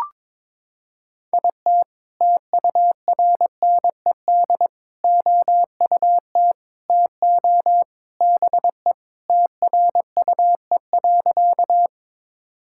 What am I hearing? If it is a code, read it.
Morse code, heard as a keyed tone, about 22 wpm, 700 Hz. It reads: IT TURNED OUT TO BE TRUE.
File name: practice / 22wpm / Sentences with 3-Word Phrases - 22wpm